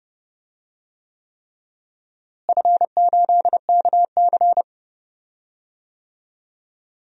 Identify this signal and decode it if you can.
Morse code, heard as a keyed tone, about 30 wpm, 700 Hz. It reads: F8KC
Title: DXCC-100 Call Signs - 4 Characters - Repeated 3x - 30wpm